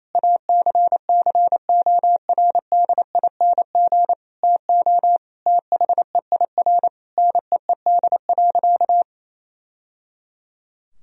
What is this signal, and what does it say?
Morse code, heard as a keyed tone, about 28 wpm, 700 Hz. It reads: ACCORDING TO THEIR NEED.
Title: Sentences from Top 1000 Words - Review Part 4 - 28wpm